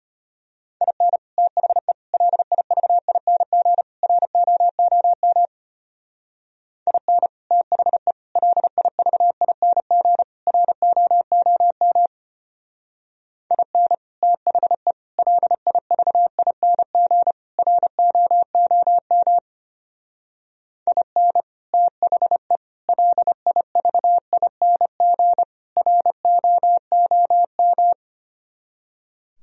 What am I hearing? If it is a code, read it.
Morse code, heard as a keyed tone, about 31 wpm, 700 Hz. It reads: IN THE LIVING ROOM IN THE LIVING ROOM IN THE LIVING ROOM IN THE LIVING ROOM
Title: Speed-Racing - Sets of 4 Words 25wpm